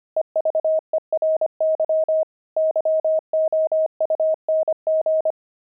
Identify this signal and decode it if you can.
Morse code, heard as a keyed tone, about 25 wpm, 630 Hz. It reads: EVERY YOUNG
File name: pangram-fs-25-cs-25-zeal